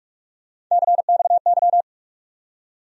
Morse code, heard as a keyed tone, about 46 wpm, 700 Hz. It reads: CXY
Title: USA Call Sign Postfixes - 3 Letters 45wpm